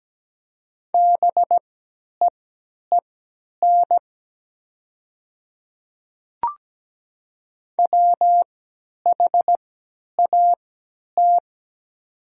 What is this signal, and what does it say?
Morse code, heard as a keyed tone, about 17 wpm, spaced as for 10 wpm, 700 Hz. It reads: BEEN WHAT
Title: 3x Character Spacing - Top 200 Words - 17wpm